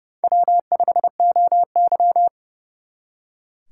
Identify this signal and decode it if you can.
Morse code, heard as a keyed tone, about 38 wpm, 700 Hz. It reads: W5OY